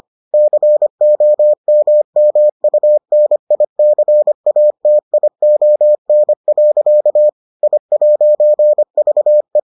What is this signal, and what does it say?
Morse code, heard as a keyed tone, about 25 wpm, 600 Hz. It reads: COMMUNICATION. I'VE